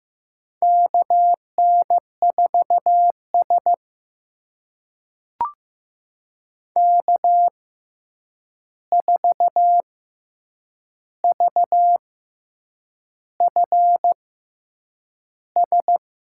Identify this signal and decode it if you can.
Morse code, heard as a keyed tone, about 15 wpm, spaced as for 7 wpm, 700 Hz. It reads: KN4S K4VFS
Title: USA Call Signs - 6x Character Spacing - 15wpm